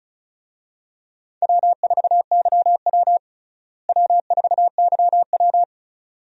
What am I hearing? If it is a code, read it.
Morse code, heard as a keyed tone, about 34 wpm, 700 Hz. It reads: W4YW W4YW
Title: USA Call Signs - 4 Characters - Repeated 2x - 35wpm